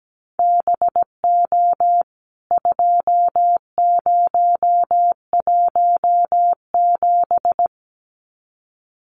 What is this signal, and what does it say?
Morse code, heard as a keyed tone, about 17 wpm, 700 Hz. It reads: BO 2017
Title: CWT Exchange 17wpm